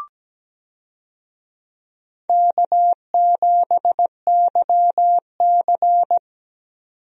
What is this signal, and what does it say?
Morse code, heard as a keyed tone, about 17 wpm, 700 Hz. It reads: K7YC
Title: USA Call Signs - 4 Characters - 17wpm